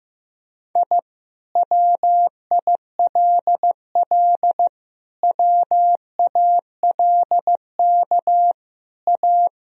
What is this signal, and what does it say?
Morse code, heard as a keyed tone, about 15 wpm, 700 Hz. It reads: I WILL WALK A
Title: Sentences from Top 1000 Words - Review Part 1 - 15wpm